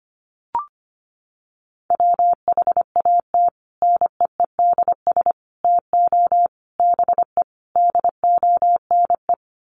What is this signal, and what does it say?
Morse code, heard as a keyed tone, about 25 wpm, 700 Hz. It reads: WHAT NEEDS TO BE DONE
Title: Sets of 5 Words Encore 25wpm